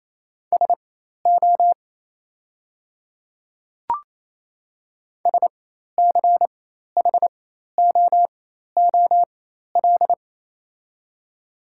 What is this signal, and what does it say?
Morse code, heard as a keyed tone, about 28 wpm, spaced as for 13 wpm, 700 Hz. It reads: SO SCHOOL
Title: 4x Character Spacing - Top 500 Words - 28wpm